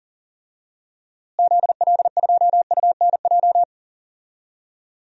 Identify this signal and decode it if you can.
Morse code, heard as a keyed tone, about 40 wpm, 700 Hz. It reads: ZL2UNJ
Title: DXCC-100 Call Signs - 6 Characters - Repeated 3x - 40wpm